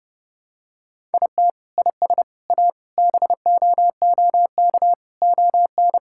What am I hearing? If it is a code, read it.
Morse code, heard as a keyed tone, about 30 wpm, 700 Hz. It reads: IT IS A BOOK ON
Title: Sentences from Top 1000 Words - Review Part 5 - 30wpm